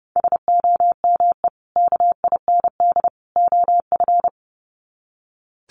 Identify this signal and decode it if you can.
Morse code, heard as a keyed tone, about 30 wpm, 700 Hz. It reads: SOME KIND OF
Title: Sets of 3 Words 30wpm